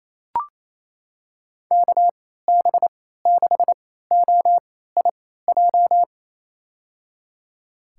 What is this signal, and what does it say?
Morse code, heard as a keyed tone, about 28 wpm, spaced as for 16 wpm, 700 Hz. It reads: KB6OIJ
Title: USA Call Signs - 6 Characters - 3x Character Spacing - 28wpm